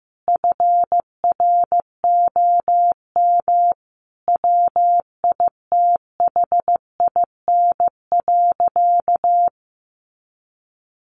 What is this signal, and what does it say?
Morse code, heard as a keyed tone, about 15 wpm, 700 Hz. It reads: FROM WITHIN.
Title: Sentences from Top 700 Words 15wpm